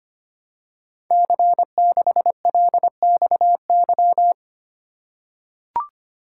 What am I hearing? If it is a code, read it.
Morse code, heard as a keyed tone, about 25 wpm, 700 Hz. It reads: C6LXY
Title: Code Groups - 5 Letter:Numbers - 25wpm